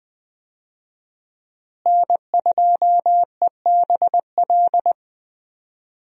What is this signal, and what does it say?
Morse code, heard as a keyed tone, about 21 wpm, 700 Hz. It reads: N2EBL
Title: USA Call Signs - 5 Characters - Repeated 2x - 20wpm